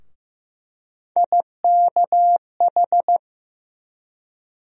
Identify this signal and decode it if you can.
Morse code, heard as a keyed tone, about 15 wpm, 700 Hz. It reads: IKH
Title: USA Call Sign Postfixes - 3 Letters 15wpm